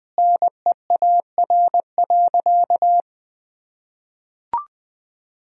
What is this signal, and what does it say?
Morse code, heard as a keyed tone, about 20 wpm, 700 Hz. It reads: NEAR.
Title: Sentences from Top 300 Words and Abbreviations 20wpm